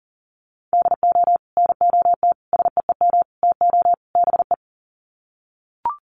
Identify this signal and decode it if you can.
Morse code, heard as a keyed tone, about 40 wpm, 700 Hz. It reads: DO NOT SEEM TO BE
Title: Sets of 5 Words Encore2 40wpm